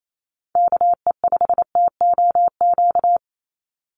Morse code, heard as a keyed tone, about 28 wpm, 700 Hz. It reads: KE5TOQ
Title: USA Call Signs 28wpm